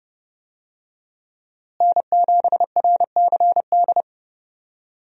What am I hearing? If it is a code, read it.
Morse code, heard as a keyed tone, about 30 wpm, 700 Hz. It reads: N7RCD
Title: USA Call Signs - 5 Characters - Repeated 3x - 30wpm